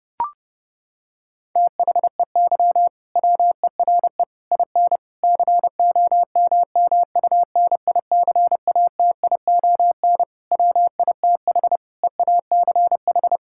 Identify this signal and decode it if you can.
Morse code, heard as a keyed tone, about 32 wpm, 700 Hz. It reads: THEY WERE IN COMMUNICATION WITH EACH
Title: Sentences from Top 1000 Words - Review Part 6 - 30wpm